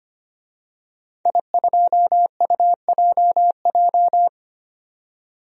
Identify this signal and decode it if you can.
Morse code, heard as a keyed tone, about 25 wpm, 700 Hz. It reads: I2UJJ
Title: DXCC-100 Call Signs - 5 Characters - 25wpm